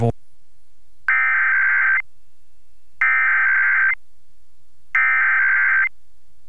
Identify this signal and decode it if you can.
Emergency Alert System (SAME) tones (header bursts), heard as harsh buzzing bursts, with speech. It is heard from 1.1 to 5.8 s.